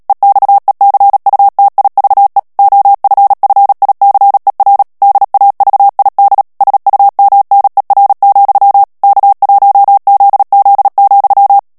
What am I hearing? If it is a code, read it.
Morse code, heard as a keyed tone, about 37 wpm, 800 Hz. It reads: EXECUTIVE OFFICER DAVID SUMNER, K1ZZ,
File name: morse mp3 / ANGLAIS texte clair 3 vitesse 35 mots minute